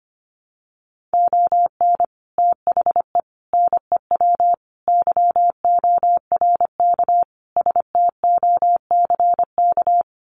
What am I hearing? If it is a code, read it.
Morse code, heard as a keyed tone, about 25 wpm, 700 Hz. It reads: ON THE NEW YORK STOCK
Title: Sets of 5 Words Encore 25wpm